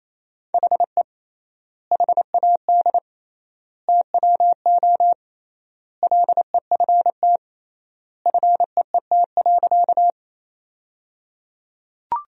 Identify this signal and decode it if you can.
Morse code, heard as a keyed tone, about 28 wpm, 700 Hz. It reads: HE HAD TWO LEFT FEET.
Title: 3x Word-Spacing - Sentences from Top 300 Words - 28wpm